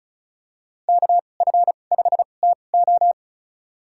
Morse code, heard as a keyed tone, about 34 wpm, spaced as for 25 wpm, 700 Hz. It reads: KF5TO
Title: USA Call Signs - 5 Characters - 2x Character Spacing - 35wpm